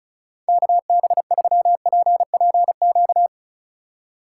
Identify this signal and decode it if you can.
Morse code, heard as a keyed tone, about 36 wpm, 700 Hz. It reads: KB3PPQ
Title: USA Call Signs - 6 Characters - 3x Character Spacing - 35wpm